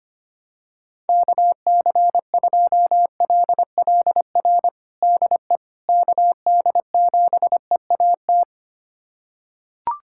Morse code, heard as a keyed tone, about 25 wpm, 700 Hz. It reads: KC2LLR DE KD7EAT
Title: USA Callsign DE Callsign - 6 Characters - 25wpm